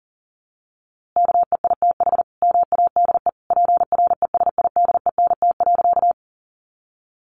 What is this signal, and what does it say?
Morse code, heard as a keyed tone, about 40 wpm, 700 Hz. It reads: KEITH MADE PRESIDENT.
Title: A Sound Of Thunder 40wpm